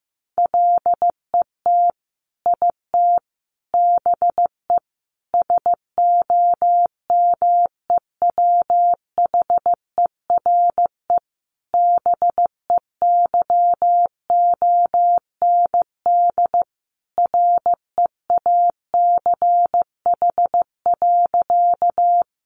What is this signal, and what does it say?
Morse code, heard as a keyed tone, about 15 wpm, 700 Hz. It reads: LET IT BE SOMEWHERE BEYOND REACH.
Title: Sentences from Top 1400 Words 15wpm